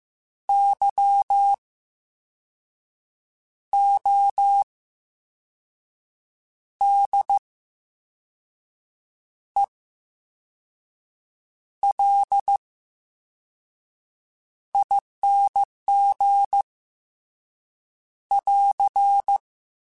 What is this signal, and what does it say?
Morse code, heard as a keyed tone, about 15 wpm, spaced as for 5 wpm, 800 Hz. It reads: YODELING <AR>
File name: Special INGs / Special INGs-mp3s / Slow ing @13